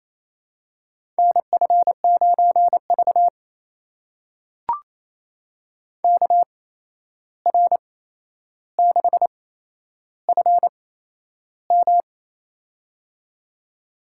Morse code, heard as a keyed tone, about 28 wpm, spaced as for 11 wpm, 700 Hz. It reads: NF9V KR6FM
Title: USA Call Signs - 8x Character Spacing - 28wpm